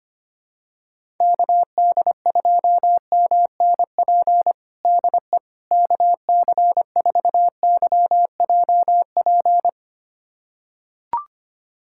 Morse code, heard as a keyed tone, about 25 wpm, 700 Hz. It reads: KD2MNP DE KC4YJP
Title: USA Callsign DE Callsign 25wpm